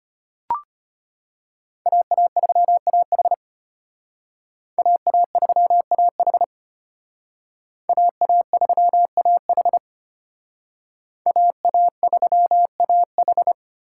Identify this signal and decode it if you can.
Morse code, heard as a keyed tone, about 32 wpm, 700 Hz. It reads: AA3AH AA3AH AA3AH AA3AH